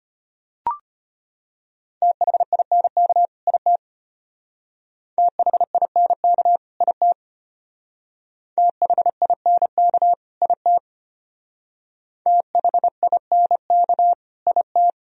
Morse code, heard as a keyed tone, about 32 wpm, 700 Hz. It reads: THINK IT THINK IT THINK IT THINK IT